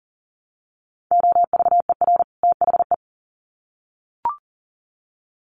Morse code, heard as a keyed tone, about 40 wpm, 700 Hz. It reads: OVER THE
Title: Sets of 2 Words 40wpm